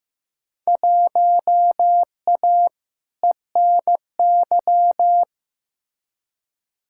Morse code, heard as a keyed tone, about 15 wpm, 700 Hz. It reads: 1A ENY
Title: Field Day Exchanges Encore - 15wpm